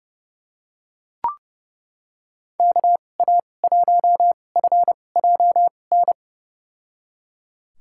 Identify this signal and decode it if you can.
Morse code, heard as a keyed tone, about 30 wpm, spaced as for 22 wpm, 700 Hz. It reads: KA1FJN
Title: USA Call Signs - 6 Characters - 2x Character Spacing - 30wpm